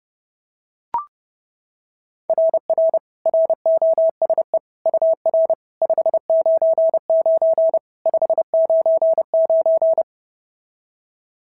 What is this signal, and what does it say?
Morse code, heard as a keyed tone, about 27 wpm, 650 Hz. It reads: RR ROSE UR 599 599
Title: SOTA QSO Practice 30wpm